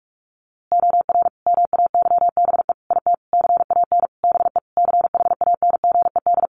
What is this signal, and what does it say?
Morse code, heard as a keyed tone, about 44 wpm, 700 Hz. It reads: OR MAYBE IT CAN BE CHANGED
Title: A Sound Of Thunder 45wpm